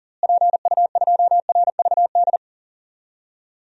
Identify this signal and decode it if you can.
Morse code, heard as a keyed tone, about 40 wpm, 700 Hz. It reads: PU2RVD
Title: DXCC-100 Call Signs - 6 Characters - Repeated 2x - 40wpm